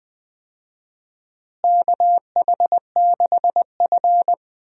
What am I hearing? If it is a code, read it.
Morse code, heard as a keyed tone, about 21 wpm, 700 Hz. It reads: KH6F
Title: DXCC-100 Call Signs - 4 Characters - Repeated 2x - 20wpm